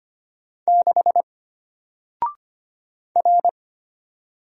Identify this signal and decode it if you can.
Morse code, heard as a keyed tone, about 25 wpm, 700 Hz. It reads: 6 R
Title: Single Letter-Number Alt 25wpm